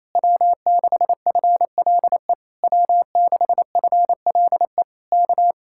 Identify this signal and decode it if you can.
Morse code, heard as a keyed tone, about 26 wpm, 700 Hz. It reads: W6FLE W6FLE K
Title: Speed-Racing Common QSO Phrases Encore - 28wpm